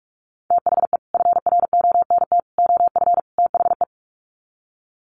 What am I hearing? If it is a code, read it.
Morse code, heard as a keyed tone, about 44 wpm, 700 Hz. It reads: THE FRONT OF THE
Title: Sets of 4 Words 45wpm